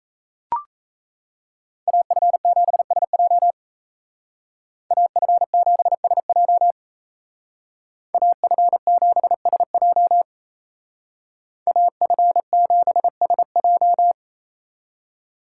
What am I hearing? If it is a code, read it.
Morse code, heard as a keyed tone, about 33 wpm, 700 Hz. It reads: AF7SJ AF7SJ AF7SJ AF7SJ
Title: Speed Racing - USA 2x2 Call Signs 28wpm